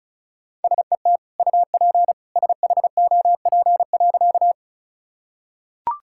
Morse code, heard as a keyed tone, about 34 wpm, 700 Hz. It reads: SET UP SHOP.
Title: Sentences from Top 1600 Words 35wpm